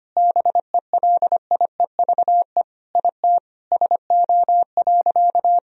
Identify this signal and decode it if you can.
Morse code, heard as a keyed tone, about 25 wpm, 700 Hz. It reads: BELIEVE IT SO.
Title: Sentences from Top 1000 Words - Review Part 4 - 25wpm